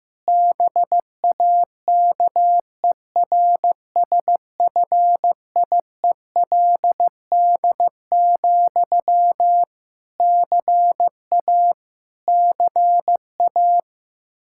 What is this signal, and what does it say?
Morse code, heard as a keyed tone, about 15 wpm, 700 Hz. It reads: BAKERSFIELD, CA CA
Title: K1USN SST QSO Practice 15wpm